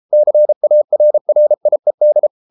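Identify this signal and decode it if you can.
Morse code, heard as a keyed tone, about 33 wpm, 600 Hz. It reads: CARRIED
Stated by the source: Morse code practice at 32 wpm from CW med Gnister, reading Ken Follett's novel Never in English.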